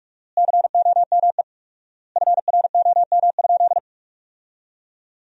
Morse code, heard as a keyed tone, about 42 wpm, 700 Hz. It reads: COME FROM?
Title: 4x Word-Spacing - Sentences from Top 300 Words - 45wpm